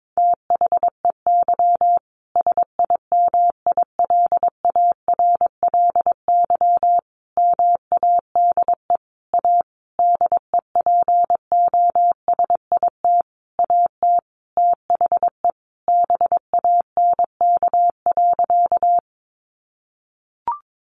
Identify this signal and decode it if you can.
Morse code, heard as a keyed tone, about 22 wpm, 700 Hz. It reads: THEY SIMILARLY MADE A DEPOSIT AT THE BANK.
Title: Sentences from Top 1400 Words 22wpm